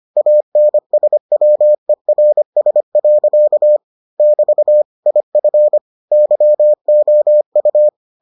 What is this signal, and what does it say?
Morse code, heard as a keyed tone, about 25 wpm, 600 Hz. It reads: ANSWERS. <BT> IF YOU